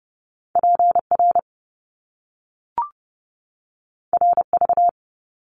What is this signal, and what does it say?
Morse code, heard as a keyed tone, about 30 wpm, 700 Hz. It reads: PR RV
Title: Top Two Letter Combinations in Words 30wpm